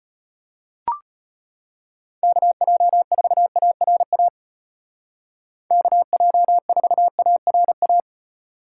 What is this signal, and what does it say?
Morse code, heard as a keyed tone, about 35 wpm, 700 Hz. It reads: KJ4ARA KJ4ARA